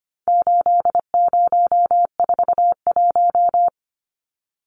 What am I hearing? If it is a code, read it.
Morse code, heard as a keyed tone, about 25 wpm, 700 Hz. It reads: 8041
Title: Numbers - 4-Digit 25wpm